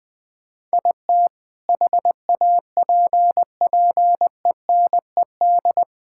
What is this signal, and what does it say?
Morse code, heard as a keyed tone, about 20 wpm, 700 Hz. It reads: IT HAPPENED
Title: Sentences with 2-Word Phrases - 20wpm